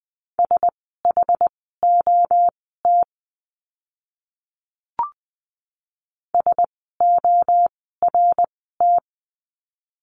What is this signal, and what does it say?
Morse code, heard as a keyed tone, about 20 wpm, spaced as for 14 wpm, 700 Hz. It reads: SHOT SORT
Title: 061 Lesson 11f - Letter h - Words 20wpm